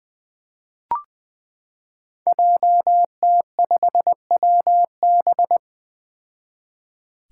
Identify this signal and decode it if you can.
Morse code, heard as a keyed tone, about 20 wpm, 700 Hz. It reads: JT5WB
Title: Five Challenging Letter-Number Combinations 20wpm